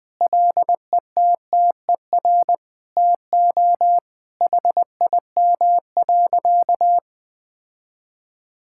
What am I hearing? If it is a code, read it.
Morse code, heard as a keyed tone, about 21 wpm, 700 Hz. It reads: LETTER TO HIM.
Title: Sentences with 2-Word Phrases - 20wpm